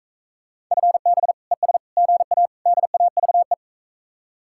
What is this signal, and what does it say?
Morse code, heard as a keyed tone, about 45 wpm, 700 Hz. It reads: FB ES GA DAVE